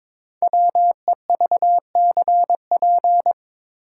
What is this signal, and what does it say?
Morse code, heard as a keyed tone, about 22 wpm, 700 Hz. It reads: WEVCP
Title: Code Groups - 5 Letter:Numbers - 22wpm